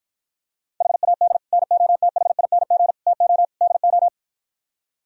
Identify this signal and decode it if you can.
Morse code, heard as a keyed tone, about 52 wpm, 700 Hz. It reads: HAD NOTHING TO DO